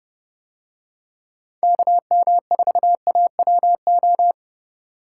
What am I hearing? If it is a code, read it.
Morse code, heard as a keyed tone, about 30 wpm, 700 Hz. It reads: KM4AWO